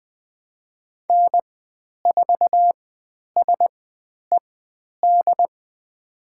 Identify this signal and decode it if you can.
Morse code, heard as a keyed tone, about 20 wpm, spaced as for 10 wpm, 700 Hz. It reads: N4SED
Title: 095 Lesson 16g - Letter c - Call Signs 20wpm